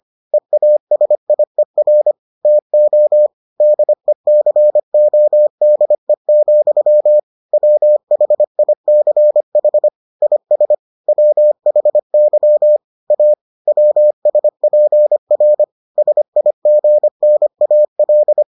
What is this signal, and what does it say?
Morse code, heard as a keyed tone, about 25 wpm, 600 Hz. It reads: EASIER TO DECODE, WHICH IS WHY A WSPR SIGNAL